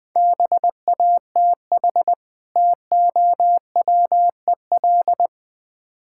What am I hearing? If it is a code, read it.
Morse code, heard as a keyed tone, about 20 wpm, 700 Hz. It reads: BATH TOWEL
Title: Easy Words - Speed-Racing with Hints - 20wpm